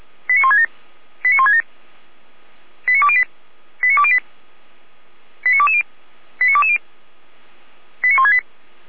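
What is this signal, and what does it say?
Five-tone selective call (ZVEI), heard as quick runs of five tones, tones of about 70 ms. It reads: 89167 89167 89208 89208 89220 89220 89168
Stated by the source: ZVEI five-tone selective call received on SDR.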